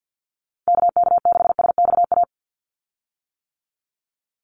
Morse code, heard as a keyed tone, about 48 wpm, 700 Hz. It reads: KK6SXA
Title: USA Call Signs 50wpm